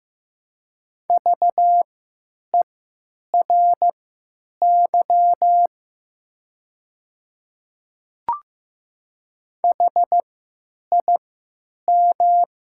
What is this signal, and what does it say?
Morse code, heard as a keyed tone, about 15 wpm, spaced as for 9 wpm, 700 Hz. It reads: VERY HIM
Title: 3x Character Spacing - Top 200 Words - 15wpm